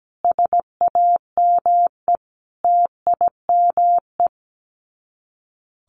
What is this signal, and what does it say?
Morse code, heard as a keyed tone, about 17 wpm, 700 Hz. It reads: SAME TIME
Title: Sets of 5 Words Encore2 17wpm